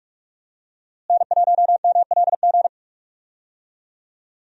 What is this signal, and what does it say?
Morse code, heard as a keyed tone, about 45 wpm, 700 Hz. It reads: N1MLG
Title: USA Call Signs - 5 Characters - Repeated 2x - 45wpm